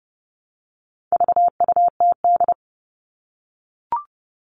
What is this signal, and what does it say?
Morse code, heard as a keyed tone, about 30 wpm, 700 Hz. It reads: VUTD
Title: All Cut Numbers - 4-Digit 30wpm